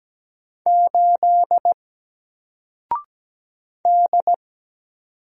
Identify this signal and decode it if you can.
Morse code, heard as a keyed tone, about 17 wpm, 700 Hz. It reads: 8 D
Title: Single Letter-Number Alt 17wpm